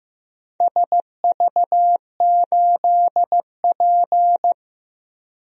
Morse code, heard as a keyed tone, about 15 wpm, 700 Hz. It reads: SV8P
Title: DXCC-100 Call Signs - 4 Characters - Repeated 2x - 15wpm